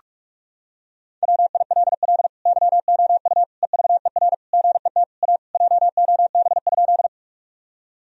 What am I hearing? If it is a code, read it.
Morse code, heard as a keyed tone, about 43 wpm, 700 Hz. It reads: WILL YOU EVER GET A JOB?
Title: Sentences from Top 1000 Words - Review Part 3 - 45wpm